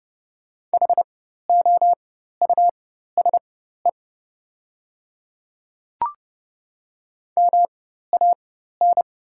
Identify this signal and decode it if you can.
Morse code, heard as a keyed tone, about 31 wpm, spaced as for 14 wpm, 690 Hz. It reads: HOUSE MAN
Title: 4x Character Spacing - Top 300 Words - 30wpm